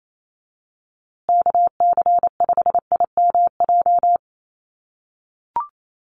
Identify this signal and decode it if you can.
Morse code, heard as a keyed tone, about 28 wpm, 700 Hz. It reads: KC5IMJ
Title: USA Call Signs Repeated 3x 28wpm